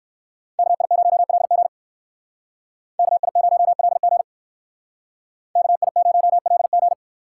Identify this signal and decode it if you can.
Morse code, heard as a keyed tone, about 64 wpm, 690 Hz. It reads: KI0LG KI0LG KI0LG